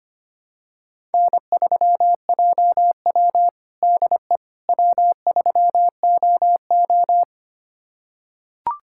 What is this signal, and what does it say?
Morse code, heard as a keyed tone, about 25 wpm, 700 Hz. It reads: N3JW DE W3OO
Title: USA Callsign DE Callsign - 4 Characters - 25wpm